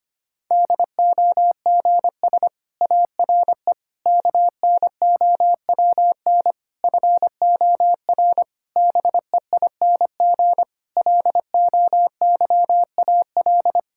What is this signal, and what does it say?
Morse code, heard as a keyed tone, about 25 wpm, 700 Hz. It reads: DOGS ARE KNOWN FOR BEING LOYAL
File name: practice / 25wpm / Difficult Sentences with Hints - 25wpm